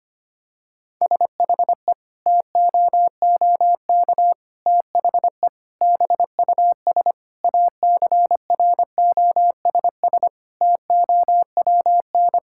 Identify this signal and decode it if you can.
Morse code, heard as a keyed tone, about 25 wpm, 700 Hz. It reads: SHE TOOK THE BUS ACROSS TOWN
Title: Sentences from Top 1000 Words - Review Part 6 - 25wpm